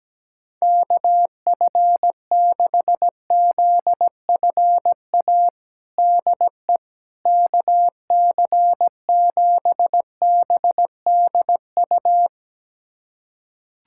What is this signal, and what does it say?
Morse code, heard as a keyed tone, about 17 wpm, 700 Hz. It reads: KF6ZFA DE KC7BDU
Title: USA Callsign DE Callsign - 6 Characters - 17wpm